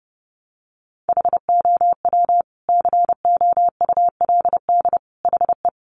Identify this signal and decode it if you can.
Morse code, heard as a keyed tone, about 30 wpm, 700 Hz. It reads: HOW COULD HE
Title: Sentences from Top 1300 Words 30wpm